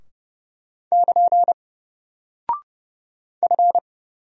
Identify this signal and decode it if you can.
Morse code, heard as a keyed tone, about 30 wpm, 700 Hz. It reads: <KN> F
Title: Single Character (Letter:Number:Punctuation:Prosign) Alt 30wpm